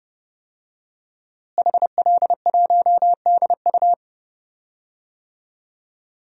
Morse code, heard as a keyed tone, about 30 wpm, 700 Hz. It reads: HL1DU
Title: DXCC-100 Call Signs - 5 Characters - Repeated 2x - 30wpm